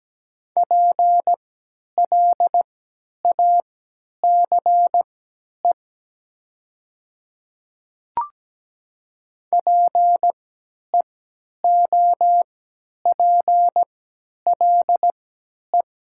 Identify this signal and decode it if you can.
Morse code, heard as a keyed tone, about 17 wpm, spaced as for 10 wpm, 700 Hz. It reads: PLACE PEOPLE
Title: 3x Character Spacing - Top 300 Words - 17wpm